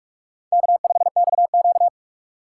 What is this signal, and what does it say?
Morse code, heard as a keyed tone, about 46 wpm, 700 Hz. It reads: K5XQ
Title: K1USN SST QSO Practice - 45wpm